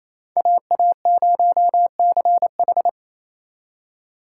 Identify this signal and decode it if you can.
Morse code, heard as a keyed tone, about 35 wpm, 700 Hz. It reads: AA0CH
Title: Speed Racing - USA 2x2 Call Signs 28wpm